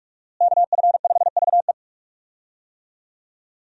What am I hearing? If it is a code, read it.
Morse code, heard as a keyed tone, about 46 wpm, 700 Hz. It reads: KF5VE
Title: USA Call Signs - 5 Characters - Repeated 2x - 45wpm